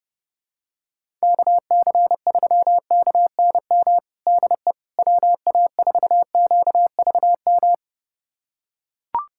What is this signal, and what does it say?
Morse code, heard as a keyed tone, about 30 wpm, 700 Hz. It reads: KC3KNM DE WA4QVM
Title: USA Callsign DE Callsign - 6 Characters - 30wpm